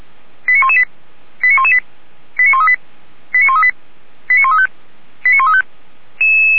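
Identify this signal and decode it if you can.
Five-tone selective call (ZVEI), heard as quick runs of five tones, tones of about 70 ms. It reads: 89118 89118 89138 89138 89135 89135
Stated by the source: Five-tone ZVEI selective call received on SDR.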